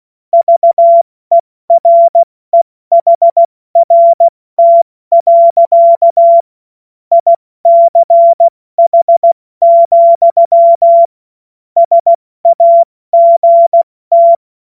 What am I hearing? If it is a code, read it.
Morse code, heard as a keyed tone, about 16 wpm, spaced as for 14 wpm, 680 Hz. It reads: VEREHRT. ICH, SAGT
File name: Morsekurs / Witze / Witz 04. Januar 2022, 70 F80